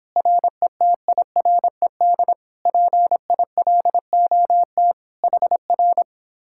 Morse code, heard as a keyed tone, about 31 wpm, 700 Hz. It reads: RETIRED PILOT HR